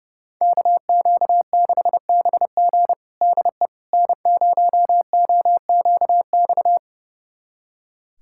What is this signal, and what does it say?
Morse code, heard as a keyed tone, about 32 wpm, 700 Hz. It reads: KQ6BG DE N0OQX
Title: USA Callsign DE Callsign - 5 Characters - 30wpm